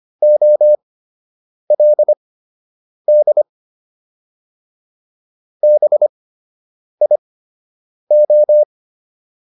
Morse code, heard as a keyed tone, about 25 wpm, spaced as for 8 wpm, 600 Hz. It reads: OLD BIO